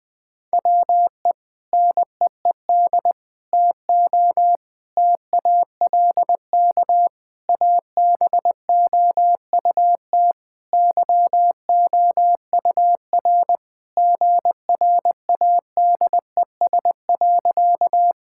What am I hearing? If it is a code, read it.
Morse code, heard as a keyed tone, about 19 wpm, 700 Hz. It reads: WE NEED TO TALK ABOUT YOUR GRADES.
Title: Sentences with 3-Word Phrases - 20wpm